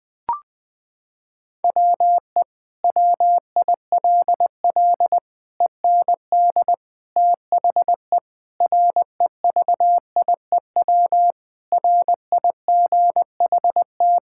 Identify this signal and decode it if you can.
Morse code, heard as a keyed tone, about 20 wpm, 700 Hz. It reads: WE WILL END THE REVIEW RIGHT
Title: Sentences from Top 1000 Words - Review Part 6 - 20wpm